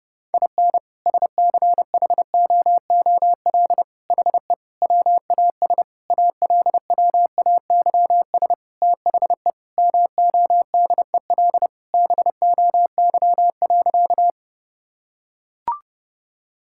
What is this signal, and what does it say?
Morse code, heard as a keyed tone, about 32 wpm, 700 Hz. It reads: IN SCHOOL HE WAS ALWAYS THE MODEL BOY.
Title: Sentences from Top 1000 Words - Review Part 5 - 30wpm